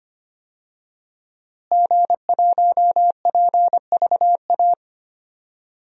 Morse code, heard as a keyed tone, about 25 wpm, 700 Hz. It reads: G1PVA
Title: DXCC-100 Call Signs - 5 Characters - Repeated 2x - 25wpm